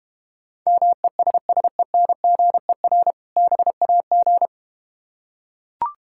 Morse code, heard as a keyed tone, about 31 wpm, 700 Hz. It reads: MESSENGER BAG
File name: practice / 25wpm / Difficult Words - Speed-Racing with Hints - 25wpm